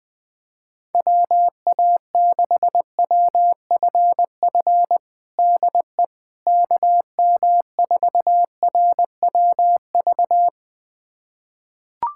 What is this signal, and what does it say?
Morse code, heard as a keyed tone, about 20 wpm, 700 Hz. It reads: WA6WFF DE KM4RWV
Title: USA Callsign DE Callsign 20wpm